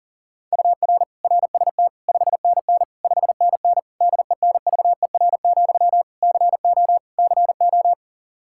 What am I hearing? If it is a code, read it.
Morse code, heard as a keyed tone, about 37 wpm, 700 Hz. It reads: UR RST 5NN 5NN DENVER, CO CO
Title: Practice QSO - Exchange 1 - Beginning - 40wpm